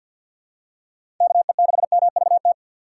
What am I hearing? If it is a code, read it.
Morse code, heard as a keyed tone, about 51 wpm, 700 Hz. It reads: KE6MVT
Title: USA Call Signs - 6 Characters - 6x Character Spacing - 50wpm